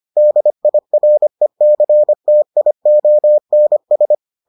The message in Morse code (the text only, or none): DIRECTIONS